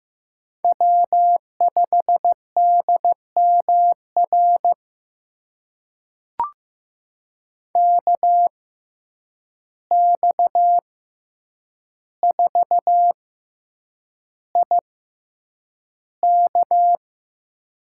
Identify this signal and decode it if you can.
Morse code, heard as a keyed tone, about 15 wpm, spaced as for 8 wpm, 700 Hz. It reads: W5DMR KX4IK
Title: USA Call Signs - 5 Characters - 6x Character Spacing - 15wpm